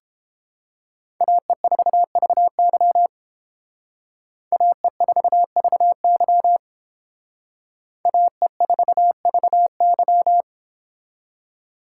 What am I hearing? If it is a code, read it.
Morse code, heard as a keyed tone, about 28 wpm, 700 Hz. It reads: AE4VY AE4VY AE4VY
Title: Speed Racing - USA 2x2 Call Signs 22wpm